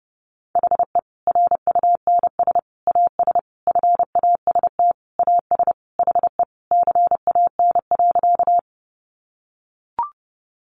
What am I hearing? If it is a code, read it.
Morse code, heard as a keyed tone, about 30 wpm, 700 Hz. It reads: HE RUNS AS FAST AS HE CAN.
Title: Additional Easy Phrases 30wpm